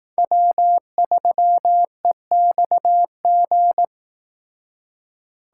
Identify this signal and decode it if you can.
Morse code, heard as a keyed tone, about 18 wpm, 700 Hz. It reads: W3EXG
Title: Speed Racing - USA 1x3 Call Signs 15wpm